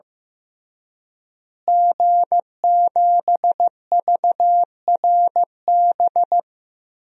Morse code, heard as a keyed tone, about 15 wpm, 700 Hz. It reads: G7VRB
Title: DXCC-100 Call Signs - 5 Characters - Repeated 3x - 15wpm